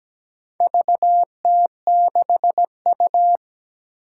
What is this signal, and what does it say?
Morse code, heard as a keyed tone, about 17 wpm, 700 Hz. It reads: VT6U
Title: Four Challenging Letter-Number Combinations 17wpm